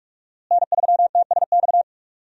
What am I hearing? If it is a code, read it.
Morse code, heard as a keyed tone, about 45 wpm, 700 Hz. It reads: N3TSX
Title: USA Call Signs - 5 Characters - Repeated 2x - 45wpm